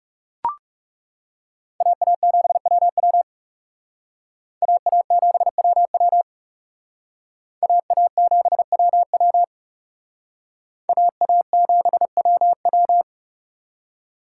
Morse code, heard as a keyed tone, about 38 wpm, 700 Hz. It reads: AA7WW AA7WW AA7WW AA7WW